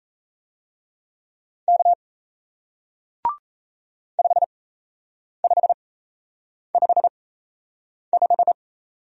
Morse code, heard as a keyed tone, about 38 wpm, spaced as for 8 wpm, 700 Hz. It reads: K 5555